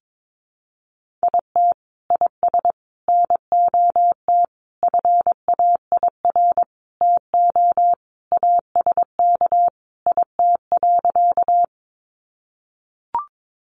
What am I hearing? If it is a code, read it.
Morse code, heard as a keyed tone, about 22 wpm, 700 Hz. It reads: IT IS NOT FAIR TO ASK IT.
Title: Sentences from Top 1300 Words 22wpm